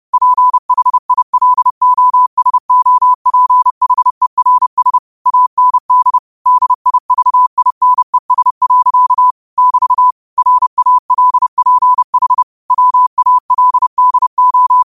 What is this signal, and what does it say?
Morse code, heard as a keyed tone, about 30 wpm, 1000 Hz. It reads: PHILOSOPHERS AND DIVINES. <BT> RALPH WALDO
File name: qotdfiles / 30wpm